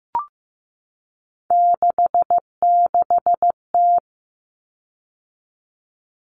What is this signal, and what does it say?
Morse code, heard as a keyed tone, about 15 wpm, 700 Hz. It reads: 66T
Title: Common Cut Numbers - 3-Digit 15wpm